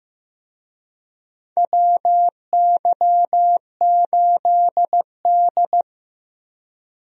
Morse code, heard as a keyed tone, about 15 wpm, 700 Hz. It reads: WY8D
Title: USA Call Signs - 4 Characters - Repeated 3x - 15wpm